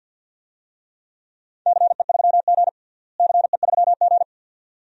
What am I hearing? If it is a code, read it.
Morse code, heard as a keyed tone, about 48 wpm, 700 Hz. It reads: CE3G CE3G
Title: DXCC-100 Call Signs - 4 Characters - Repeated 2x - 50wpm